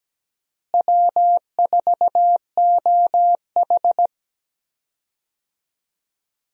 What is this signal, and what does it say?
Morse code, heard as a keyed tone, about 17 wpm, 700 Hz. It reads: W4OH
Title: USA Call Signs - 4 Characters - Repeated 3x - 17wpm